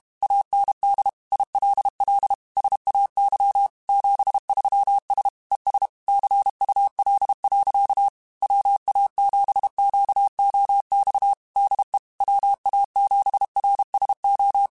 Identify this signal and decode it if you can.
Morse code, heard as a keyed tone, about 32 wpm, 800 Hz. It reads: AND ILL SAY 73S ES CUL. WA7QOX DE WA7RSO
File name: QSOs / QSOs-mp3s / WA7QOX 2@30